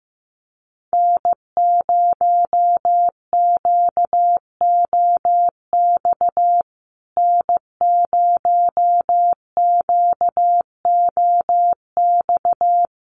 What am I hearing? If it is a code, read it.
Morse code, heard as a keyed tone, about 15 wpm, 700 Hz. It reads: N0QOX N0QOX
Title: USA Call Signs Repeated 3x 15wpm